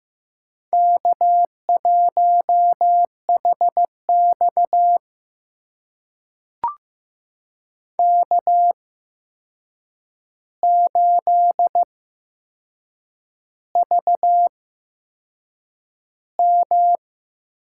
Morse code, heard as a keyed tone, about 15 wpm, spaced as for 6 wpm, 700 Hz. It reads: K1HX K8VM
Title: USA Call Signs - 4 Characters - 8x Character Spacing - 15wpm